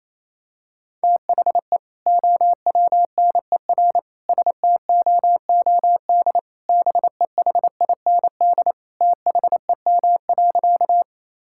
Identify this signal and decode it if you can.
Morse code, heard as a keyed tone, about 28 wpm, 700 Hz. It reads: THE OWNER STOOD BEHIND THEM.
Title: Sentences from Top 1600 Words 28wpm